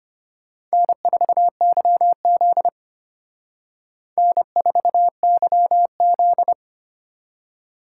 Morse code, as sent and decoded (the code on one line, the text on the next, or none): -. ....- -.-- --.. / -. ....- -.-- --..
N4YZ N4YZ